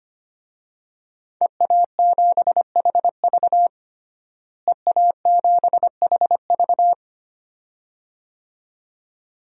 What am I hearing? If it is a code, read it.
Morse code, heard as a keyed tone, about 25 wpm, 700 Hz. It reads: EA7HV EA7HV
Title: DXCC-100 Call Signs - 5 Characters - Repeated 2x - 25wpm